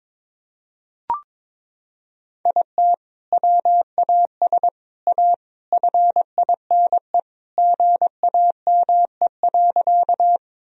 Morse code, heard as a keyed tone, about 22 wpm, 700 Hz. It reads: IT WAS A FINE GAME.
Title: Sentences from Top 1000 Words - Review Part 3 - 22wpm